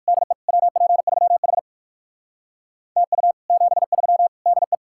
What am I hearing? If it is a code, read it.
Morse code, heard as a keyed tone, about 59 wpm, 700 Hz. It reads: DE WP3H TU 73 DE